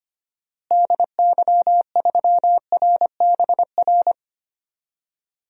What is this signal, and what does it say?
Morse code, heard as a keyed tone, about 25 wpm, 700 Hz. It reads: DY3RBR
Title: DXCC-100 Call Signs - 6 Characters - Repeated 2x - 25wpm